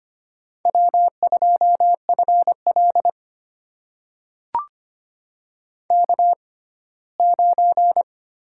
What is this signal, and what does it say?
Morse code, heard as a keyed tone, about 25 wpm, spaced as for 17 wpm, 700 Hz. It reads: W2FL K9
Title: USA Call Signs - 4 Characters - 6x Character Spacing - 25wpm